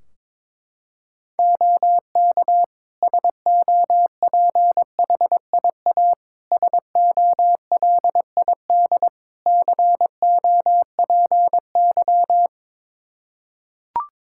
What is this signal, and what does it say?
Morse code, heard as a keyed tone, about 22 wpm, 700 Hz. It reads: OK SOPHIA SOLID COPY
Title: Practice QSO - Exchange 3 - More Info - 22wpm